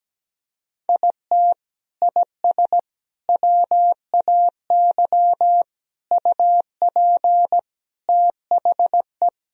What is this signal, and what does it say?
Morse code, heard as a keyed tone, about 17 wpm, 700 Hz. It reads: IT IS WAY UP THE
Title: Sentences from Top 1000 Words - Review Part 1 - 17wpm